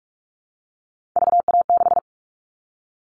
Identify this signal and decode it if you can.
Morse code, heard as a keyed tone, about 46 wpm, 700 Hz. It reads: VA6
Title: Callsign Prefixes for Top 50 DXCC - 45wpm